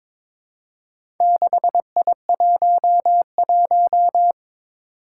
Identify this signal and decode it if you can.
Morse code, heard as a keyed tone, about 22 wpm, 700 Hz. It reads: 6I11
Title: Four Challenging Letter-Number Combinations 22wpm